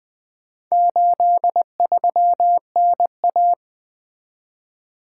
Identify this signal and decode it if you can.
Morse code, heard as a keyed tone, about 20 wpm, 700 Hz. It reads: 83NA
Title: Four Letter-Number Combinations - 20wpm